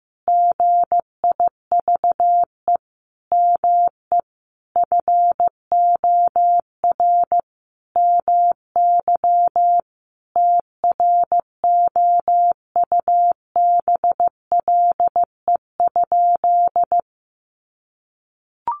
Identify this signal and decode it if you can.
Morse code, heard as a keyed tone, about 15 wpm, 700 Hz. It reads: GIVE ME FOR MY TROUBLE?
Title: Sentences from Top 1200 Words 15wpm